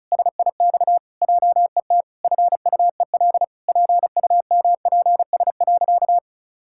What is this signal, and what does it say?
Morse code, heard as a keyed tone, about 36 wpm, 700 Hz. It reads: SIX JET FUEL PUMPS.
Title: Pangram Challenge v2 - 35wpm